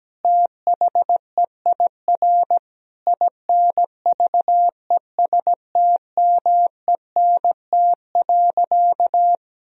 Morse code, heard as a keyed tone, about 17 wpm, 700 Hz. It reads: THEIR INVESTMENT.
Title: Sentences from Top 1000 Words - Review Part 4 - 17wpm